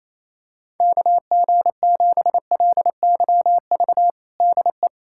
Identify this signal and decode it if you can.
Morse code, heard as a keyed tone, about 28 wpm, 700 Hz. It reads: KG7LYV DE
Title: USA Callsign DE Callsign - 6 Characters - 28wpm